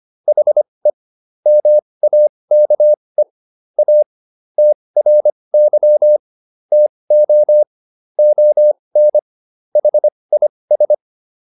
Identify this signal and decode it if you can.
Morse code, heard as a keyed tone, about 25 wpm, spaced as for 20 wpm, 600 Hz. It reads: HE MAKE A TRY TO ON HIS